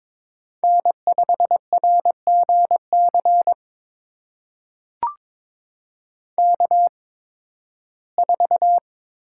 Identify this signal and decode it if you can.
Morse code, heard as a keyed tone, about 22 wpm, spaced as for 14 wpm, 700 Hz. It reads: N5RGC K4
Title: USA Call Signs - 5 Characters - 8x Character Spacing - 22wpm